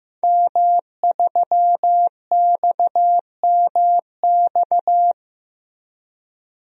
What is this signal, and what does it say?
Morse code, heard as a keyed tone, about 15 wpm, 700 Hz. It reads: M3XMX
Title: DXCC-100 Call Signs - 5 Characters - 15wpm